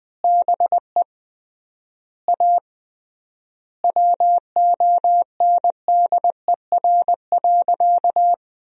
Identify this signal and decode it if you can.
Morse code, heard as a keyed tone, about 20 wpm, 700 Hz. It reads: BE A WONDER.
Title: 3x Word-Spacing - Sentences from Top 500 Words - 20wpm